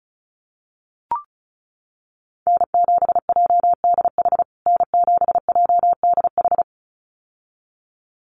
Morse code, heard as a keyed tone, about 34 wpm, 700 Hz. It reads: N7JDH N7JDH
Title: USA Call Signs Repeated 2x 35wpm